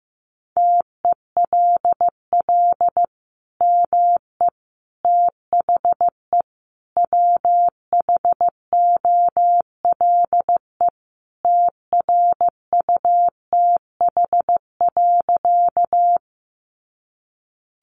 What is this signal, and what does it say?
Morse code, heard as a keyed tone, about 15 wpm, 700 Hz. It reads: TELL ME THE WHOLE TRUTH.
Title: Sentences from Top 800 Words 15wpm